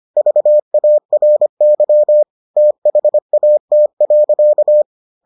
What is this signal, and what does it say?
Morse code, heard as a keyed tone, about 25 wpm, 600 Hz. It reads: VARY THAT.